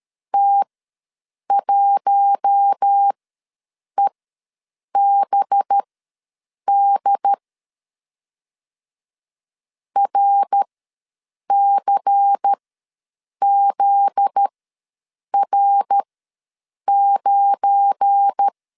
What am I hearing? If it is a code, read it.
Morse code, heard as a keyed tone, about 13 wpm, spaced as for 7 wpm, 800 Hz. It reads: T1EBD RCZR9
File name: LEZIONI / 8 YJ45 / 8 YJ ws 12wpm 3342